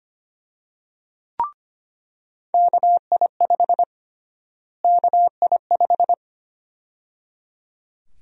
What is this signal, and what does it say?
Morse code, heard as a keyed tone, about 25 wpm, 700 Hz. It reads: KI5 KI5
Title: USA Call Sign Prefixes - Repeated 2x - 25wpm